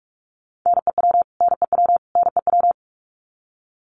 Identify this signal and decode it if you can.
Morse code, heard as a keyed tone, about 44 wpm, 700 Hz. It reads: NEW NEW NEW
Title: Top 500 Words Repeated 45wpm